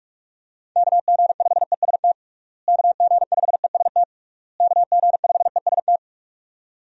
Morse code, heard as a keyed tone, about 46 wpm, 700 Hz. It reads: KG5EST KG5EST KG5EST
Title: USA Call Signs - 6 Characters - Repeated 3x - 45wpm